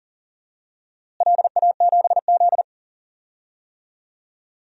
Morse code, heard as a keyed tone, about 37 wpm, 700 Hz. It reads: LA7Z
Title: DXCC-100 Call Signs - 4 Characters - Repeated 3x - 40wpm